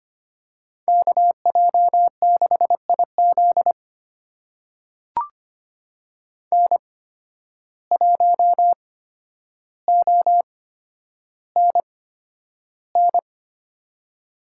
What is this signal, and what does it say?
Morse code, heard as a keyed tone, about 25 wpm, spaced as for 11 wpm, 700 Hz. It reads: KJ6IZ N1ONN
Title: USA Call Signs - 5 Characters - 8x Character Spacing - 25wpm